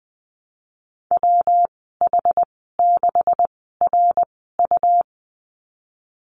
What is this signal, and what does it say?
Morse code, heard as a keyed tone, about 20 wpm, 700 Hz. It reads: W H 6 R U
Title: 154 Lesson 25g - Letter p - Call Signs 20wpm